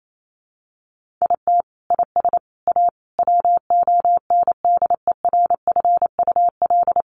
Morse code, heard as a keyed tone, about 28 wpm, 700 Hz. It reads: IT IS A WONDERFUL
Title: Sentences from Top 1500 Words 28wpm